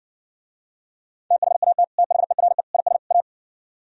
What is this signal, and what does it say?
Morse code, heard as a keyed tone, about 57 wpm, 700 Hz. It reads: THAT THERE IS A